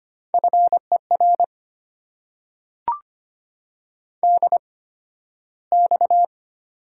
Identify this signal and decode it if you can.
Morse code, heard as a keyed tone, about 25 wpm, spaced as for 13 wpm, 700 Hz. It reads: FER DX
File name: Top 90 QSO Elements - 8x Character Spacing - 25wpm